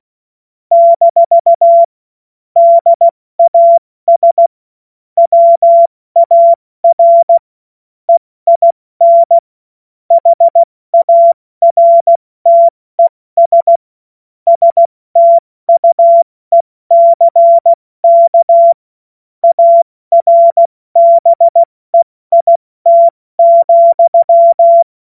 Morse code, heard as a keyed tone, about 16 wpm, spaced as for 14 wpm, 680 Hz. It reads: - DAS WAR EIN HARTES STUECK ARBEIT,
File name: Morsekurs / Witze / Witz 29. August 2023 70 F80